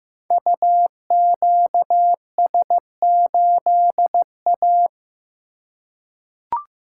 Morse code, heard as a keyed tone, about 15 wpm, 700 Hz. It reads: UQS8A
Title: Code Groups - 5 Letter:Numbers - 15wpm